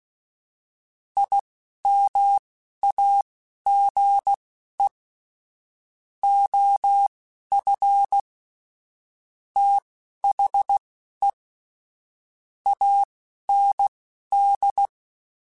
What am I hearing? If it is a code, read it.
Morse code, heard as a keyed tone, about 16 wpm, spaced as for 12 wpm, 800 Hz. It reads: IMAGE OF THE AND
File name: Technical Subjects / Technical Subjects-mp3s / OR Gate @10